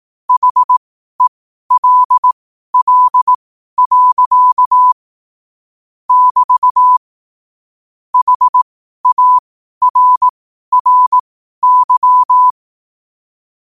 Morse code, heard as a keyed tone, about 18 wpm, spaced as for 12 wpm, 1000 Hz. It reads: HELL. <BT> HARRY
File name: qotdfiles / 12wpm